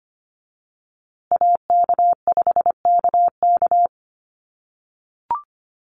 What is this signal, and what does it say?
Morse code, heard as a keyed tone, about 25 wpm, 700 Hz. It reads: AK5KK
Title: USA Call Signs Repeated 2x 25wpm